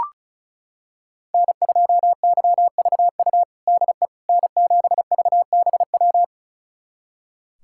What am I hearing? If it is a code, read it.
Morse code, heard as a keyed tone, about 36 wpm, 700 Hz. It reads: N2YVU DE N7VBW
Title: USA Callsign DE Callsign 35wpm